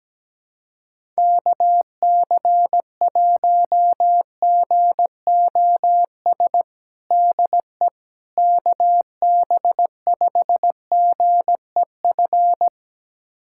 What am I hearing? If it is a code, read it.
Morse code, heard as a keyed tone, about 17 wpm, 700 Hz. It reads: KC1GOS DE KB5GEF
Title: USA Callsign DE Callsign - 6 Characters - 17wpm